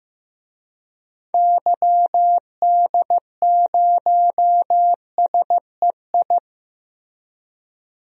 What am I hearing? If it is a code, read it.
Morse code, heard as a keyed tone, about 15 wpm, 700 Hz. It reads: YD0SEI
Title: DXCC-100 Call Signs - 6 Characters - Repeated 2x - 15wpm